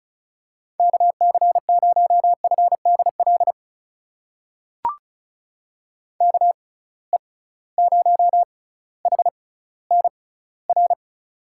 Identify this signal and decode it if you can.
Morse code, heard as a keyed tone, about 34 wpm, spaced as for 18 wpm, 700 Hz. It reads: KC0FDL KE0HNR
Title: USA Call Signs - 6 Characters - 6x Character Spacing - 35wpm